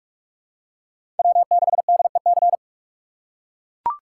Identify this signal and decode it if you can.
Morse code, heard as a keyed tone, about 42 wpm, 700 Hz. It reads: W6DEC
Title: USA Call Signs - 5 Characters - 4x Character Spacing - 45wpm